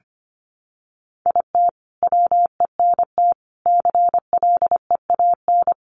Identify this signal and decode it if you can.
Morse code, heard as a keyed tone, about 25 wpm, 700 Hz. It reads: IT WENT CLEAN
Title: Sentences from Top 1200 Words 25wpm